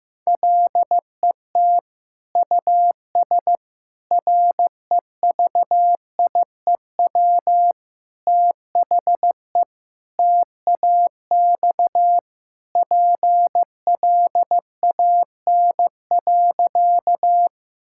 Morse code, heard as a keyed tone, about 15 wpm, 700 Hz. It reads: LET US REVIEW THE TAX PLAN.
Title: Sentences from Top 1000 Words - Review Part 6 - 15wpm